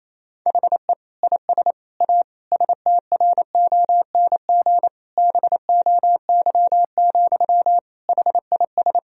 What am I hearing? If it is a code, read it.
Morse code, heard as a keyed tone, about 27 wpm, 700 Hz. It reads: HE IS A STRONG BOY, HIS
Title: Sentences from Top 1000 Words - Review Part 3 - 28wpm